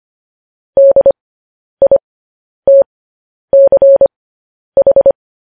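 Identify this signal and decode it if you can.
Morse code, heard as a keyed tone, about 25 wpm, spaced as for 10 wpm, 550 Hz. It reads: DITCH
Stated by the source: Morse code practice of news headlines at 10 wpm.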